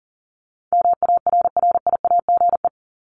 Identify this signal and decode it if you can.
Morse code, heard as a keyed tone, about 40 wpm, 700 Hz. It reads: MARRIAGE
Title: Top 751-1000 Words 40wpm